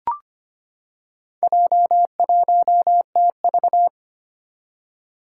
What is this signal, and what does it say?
Morse code, heard as a keyed tone, about 25 wpm, 700 Hz. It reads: J1TV
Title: Four Challenging Letter-Number Combinations 25wpm